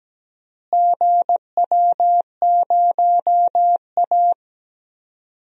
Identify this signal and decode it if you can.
Morse code, heard as a keyed tone, about 17 wpm, 700 Hz. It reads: GW0A
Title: DXCC-100 Call Signs - 4 Characters - Repeated 2x - 17wpm